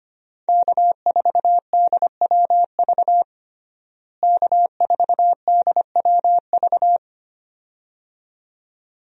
Morse code, heard as a keyed tone, about 25 wpm, 700 Hz. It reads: K4DWV K4DWV
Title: USA Call Signs - 5 Characters - Repeated 2x - 25wpm